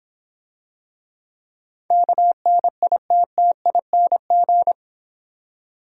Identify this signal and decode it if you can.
Morse code, heard as a keyed tone, about 26 wpm, 700 Hz. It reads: KNITTING